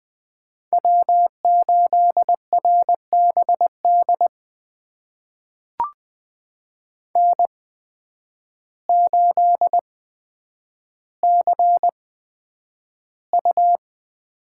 Morse code, heard as a keyed tone, about 20 wpm, spaced as for 9 wpm, 700 Hz. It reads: W8RBD N8CU
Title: USA Call Signs - 5 Characters - 8x Character Spacing - 20wpm